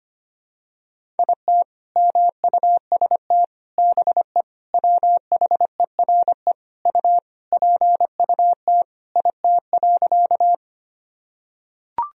Morse code, heard as a keyed tone, about 25 wpm, 700 Hz. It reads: IT MUST BE WHERE U PUT IT.
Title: Sentences from Top 200 Words and Abbreviations 25wpm